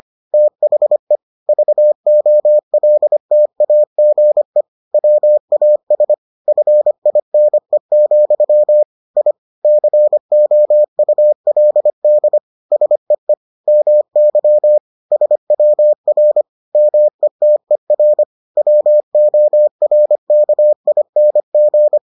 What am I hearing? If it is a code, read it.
Morse code, heard as a keyed tone, about 25 wpm, 600 Hz. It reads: THE VOLTAGE WAS FINE, I COULD SEE MY SWR METER WORKING